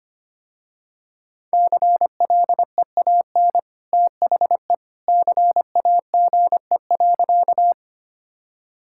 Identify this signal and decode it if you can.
Morse code, heard as a keyed tone, about 25 wpm, 700 Hz. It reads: CLEAN THE CAGE.
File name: Easy Sentences with Hints - 25wpm